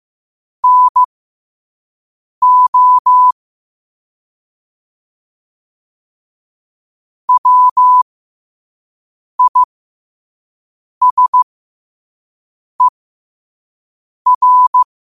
Morse code, heard as a keyed tone, about 15 wpm, spaced as for 5 wpm, 1000 Hz. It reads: NO WISER
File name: qotdfiles / 5wpm